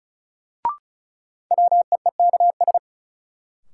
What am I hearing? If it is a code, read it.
Morse code, heard as a keyed tone, about 34 wpm, 700 Hz. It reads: WEEKS
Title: Sets of 5 Words - New Word Drill - 35wpm